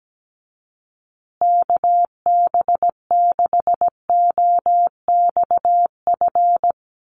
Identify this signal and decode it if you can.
Morse code, heard as a keyed tone, about 17 wpm, 700 Hz. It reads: KB6OXF
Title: USA Call Signs Repeated 2x 17wpm